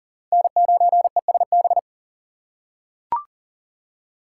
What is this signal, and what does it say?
Morse code, heard as a keyed tone, about 40 wpm, 700 Hz. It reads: N9ESB
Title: USA Call Signs - 5 Characters - 6x Character Spacing - 40wpm